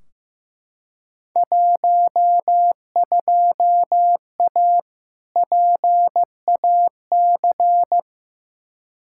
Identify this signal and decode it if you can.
Morse code, heard as a keyed tone, about 15 wpm, 700 Hz. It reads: 12A PAC
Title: Field Day Exchanges Encore - 15wpm